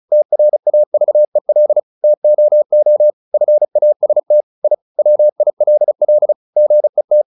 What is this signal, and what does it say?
Morse code, heard as a keyed tone, about 34 wpm, 600 Hz. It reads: TRAVEL TOO FAST I WILL GET
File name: the old man and the sea / the old man and the sea 1 35wpm